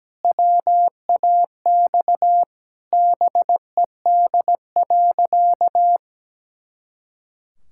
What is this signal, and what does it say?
Morse code, heard as a keyed tone, about 17 wpm, 700 Hz. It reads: WAX BED.
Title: Pangram Challenge v2 - 17wpm